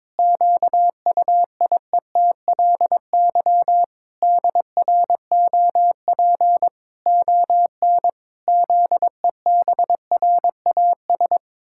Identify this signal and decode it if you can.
Morse code, heard as a keyed tone, about 22 wpm, 700 Hz. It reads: QUIETLY DROP ON ZEBRAS
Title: Pangram Challenge v2 - 22wpm